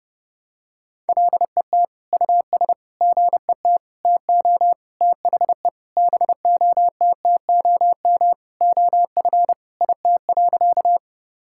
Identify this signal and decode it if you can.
Morse code, heard as a keyed tone, about 30 wpm, 700 Hz. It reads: LET US GET TO THE BOTTOM OF IT.
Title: Sentences from Top 1000 Words - Review Part 6 - 30wpm